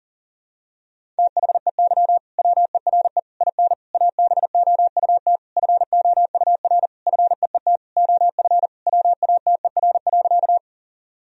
Morse code, heard as a keyed tone, about 40 wpm, 700 Hz. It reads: THEY WERE IN ABOUT FOUR FEET OF WATER.
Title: Sentences from Top 1000 Words - Review Part 1 - 40wpm